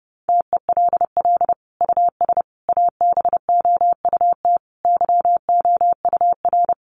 Morse code, heard as a keyed tone, about 30 wpm, 700 Hz. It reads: TELL US ABOUT YOUR
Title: Sentences from Top 1300 Words 30wpm